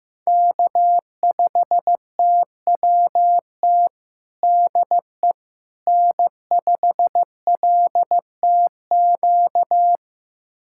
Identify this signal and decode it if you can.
Morse code, heard as a keyed tone, about 15 wpm, 700 Hz. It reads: K5TWT DE N5LTQ
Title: USA Callsign DE Callsign - 5 Characters - 15wpm